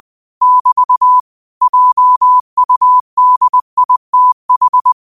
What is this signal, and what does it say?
Morse code, heard as a keyed tone, about 20 wpm, 1000 Hz. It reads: <BT> JUDITH
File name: qotdfiles / 20wpm